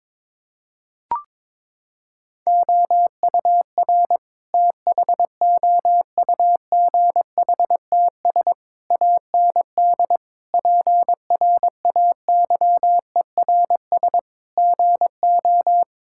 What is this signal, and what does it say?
Morse code, heard as a keyed tone, about 21 wpm, 700 Hz. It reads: OUR THOUGHTS AND PRAYERS GO
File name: practice / 22wpm / Sentences with 8-Word Phrases - 22wpm